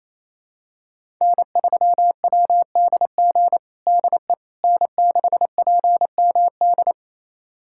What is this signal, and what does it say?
Morse code, heard as a keyed tone, about 28 wpm, 700 Hz. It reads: N3WDG DE N6PMD
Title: USA Callsign DE Callsign - 5 Characters - 28wpm